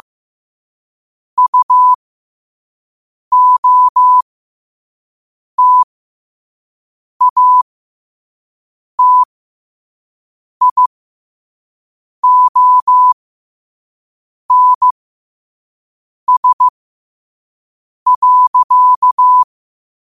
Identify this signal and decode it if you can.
Morse code, heard as a keyed tone, about 15 wpm, spaced as for 5 wpm, 1000 Hz. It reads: UOTATIONS.
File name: 5wpm